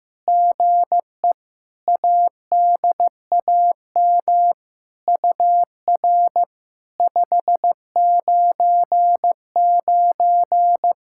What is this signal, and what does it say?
Morse code, heard as a keyed tone, about 15 wpm, 700 Hz. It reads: GE ADAM UR 599
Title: SOTA QSO Practice 15wpm